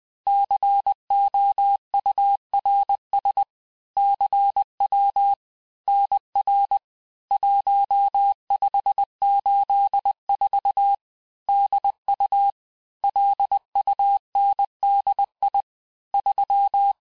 Morse code, heard as a keyed tone, about 20 wpm, 800 Hz. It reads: COURS CW NR 1584 DU LUNDI 3
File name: cours-du-lundi files / cours-1584-teste-chiffre-vitesse-19mpm